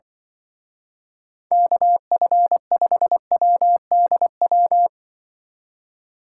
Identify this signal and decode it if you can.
Morse code, heard as a keyed tone, about 25 wpm, 700 Hz. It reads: KF5WDW
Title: Speed Racing - USA 2x3 Call Signs 20wpm